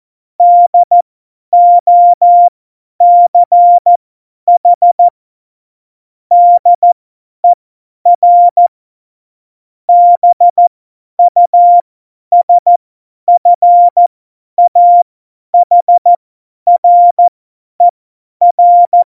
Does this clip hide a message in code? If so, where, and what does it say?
Yes: Morse code from 0.4 to 19.0 s, heard as a keyed tone, about 14 wpm, spaced as for 10 wpm, 700 Hz. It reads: DOCH DER BUSFAHRER